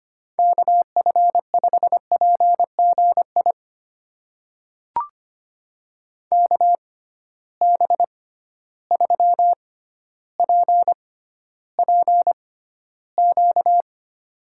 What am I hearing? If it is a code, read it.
Morse code, heard as a keyed tone, about 25 wpm, spaced as for 13 wpm, 700 Hz. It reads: KF5PGI KB3PPQ
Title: USA Call Signs - 6 Characters - 6x Character Spacing - 25wpm